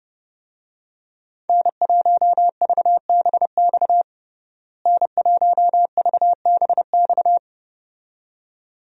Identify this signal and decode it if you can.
Morse code, heard as a keyed tone, about 32 wpm, 700 Hz. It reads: N1VBX N1VBX
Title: USA Call Signs - 5 Characters - Repeated 2x - 30wpm